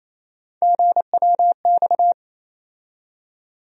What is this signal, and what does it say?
Morse code, heard as a keyed tone, about 28 wpm, 700 Hz. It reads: GWX
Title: USA Call Sign Postfixes - 3 Letters 28wpm